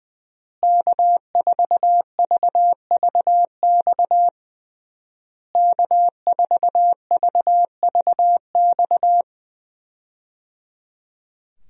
Morse code, heard as a keyed tone, about 20 wpm, 700 Hz. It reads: K4VVX K4VVX
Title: USA Call Signs - 5 Characters - Repeated 3x - 20wpm